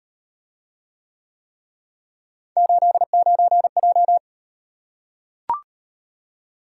Morse code, heard as a keyed tone, about 38 wpm, 700 Hz. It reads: 89J